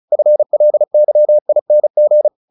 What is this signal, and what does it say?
Morse code, heard as a keyed tone, about 34 wpm, 600 Hz. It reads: FLYING
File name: the old man and the sea / the old man and the sea 1 35wpm